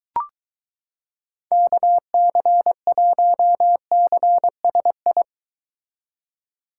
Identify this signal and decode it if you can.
Morse code, heard as a keyed tone, about 22 wpm, 700 Hz. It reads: KC1CSI